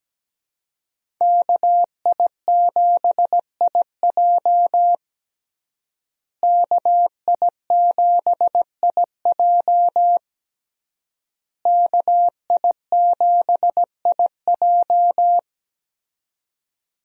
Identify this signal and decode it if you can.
Morse code, heard as a keyed tone, about 17 wpm, 700 Hz. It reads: KI7IJ KI7IJ KI7IJ
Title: USA Call Signs - 5 Characters - Repeated 3x - 17wpm